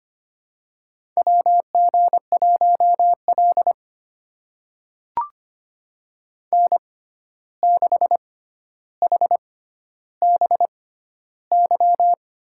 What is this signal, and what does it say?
Morse code, heard as a keyed tone, about 25 wpm, spaced as for 12 wpm, 700 Hz. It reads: WG1L N6HBY
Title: USA Call Signs - 6x Character Spacing - 25wpm